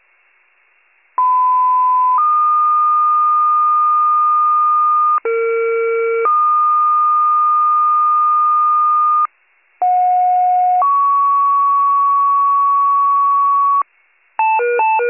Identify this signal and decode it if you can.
Two-tone page, heard as a one-second tone, then a three-second tone, about 1010 then 1220 Hz, about 460 then 1220 Hz, about 730 then 1120 Hz. It is heard from 1.2 to 13.8 s.